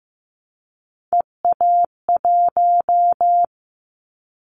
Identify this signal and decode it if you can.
Morse code, heard as a keyed tone, about 15 wpm, 700 Hz. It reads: EA1
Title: Callsign Prefixes for Top 25 DXCC - 15wpm